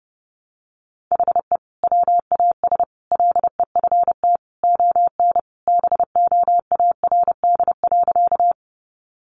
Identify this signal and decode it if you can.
Morse code, heard as a keyed tone, about 30 wpm, 700 Hz. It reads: HE WAS LEFT ON BOARD.
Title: Sentences from Top 700 Words 30wpm